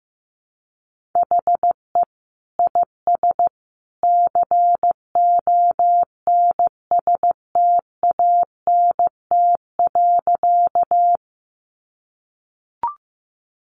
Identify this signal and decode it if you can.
Morse code, heard as a keyed tone, about 15 wpm, 700 Hz. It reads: HE IS CONSTANT.
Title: Sentences from Top 1500 Words 15wpm